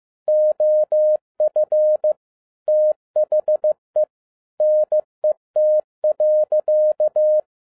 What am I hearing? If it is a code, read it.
Morse code, heard as a keyed tone, about 15 wpm, 610 Hz. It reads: OF THE NET.
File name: 2012-01-mp3 / n17-SUMMARY OF FEA NET IN 2011-15wpm